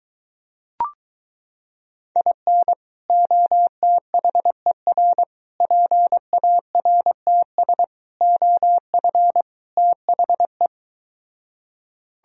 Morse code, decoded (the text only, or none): IN OTHER PARTS OF THE